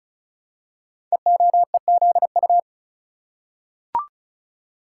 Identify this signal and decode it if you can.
Morse code, heard as a keyed tone, about 36 wpm, 700 Hz. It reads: EOEZU
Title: Code Groups - 5 Letters - 35wpm